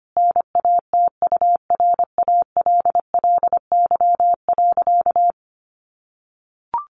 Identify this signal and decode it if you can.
Morse code, heard as a keyed tone, about 25 wpm, 700 Hz. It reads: NATURALLY.
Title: Sentences from Top 1300 Words 25wpm